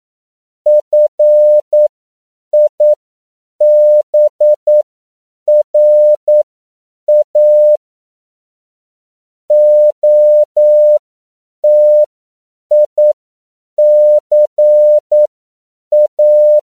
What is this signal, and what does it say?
Morse code, heard as a keyed tone, about 9 wpm, spaced as for 7 wpm, 600 Hz. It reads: FIBRA OTICA
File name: CW / 10PPM